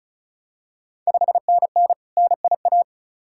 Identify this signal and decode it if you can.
Morse code, heard as a keyed tone, about 36 wpm, 700 Hz. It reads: 5NN NIA
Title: Ontario QSO Party Exchange 35wpm